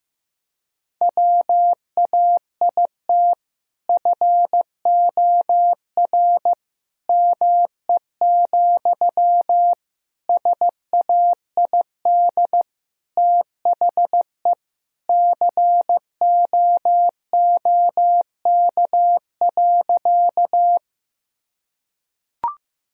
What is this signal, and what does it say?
Morse code, heard as a keyed tone, about 16 wpm, 700 Hz. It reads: WAIT FOR ME, SAID THE COOK.
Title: Sentences from Top 1000 Words - Review Part 4 - 15wpm